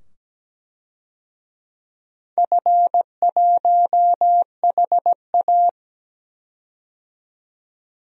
Morse code, heard as a keyed tone, about 17 wpm, 700 Hz. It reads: F1HA
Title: DXCC-100 Call Signs - 4 Characters - 17wpm